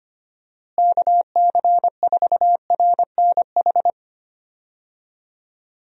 Morse code, heard as a keyed tone, about 25 wpm, 700 Hz. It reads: KC4RNH
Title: USA Call Signs - 6 Characters - Repeated 3x - 25wpm